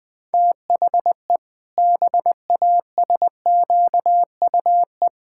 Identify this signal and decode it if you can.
Morse code, heard as a keyed tone, about 20 wpm, 700 Hz. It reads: THE BASQUE
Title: Pangram Challenge v2 - 20wpm